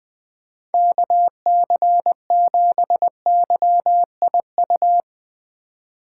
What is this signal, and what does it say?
Morse code, heard as a keyed tone, about 22 wpm, 700 Hz. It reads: KC7YIU
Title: Speed Racing - USA 2x3 Call Signs 20wpm